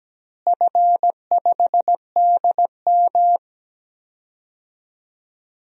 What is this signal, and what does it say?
Morse code, heard as a keyed tone, about 17 wpm, 700 Hz. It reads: F5DM
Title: DXCC-100 Call Signs - 4 Characters - Repeated 3x - 17wpm